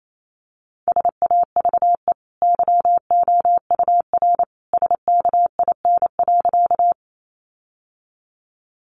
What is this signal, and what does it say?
Morse code, heard as a keyed tone, about 28 wpm, 700 Hz. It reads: SAVE YOUR SKIN.
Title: Sentences from Top 1300 Words 28wpm